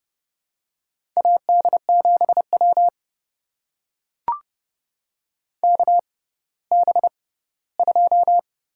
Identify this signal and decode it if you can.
Morse code, heard as a keyed tone, about 31 wpm, spaced as for 17 wpm, 700 Hz. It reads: AD7W KB2
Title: USA Call Signs - 6x Character Spacing - 30wpm